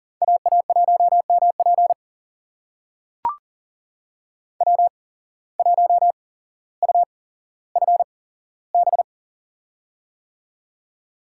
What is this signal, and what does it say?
Morse code, heard as a keyed tone, about 39 wpm, spaced as for 17 wpm, 700 Hz. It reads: AA1MP W1UFB
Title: USA Call Signs - 5 Characters - 8x Character Spacing - 40wpm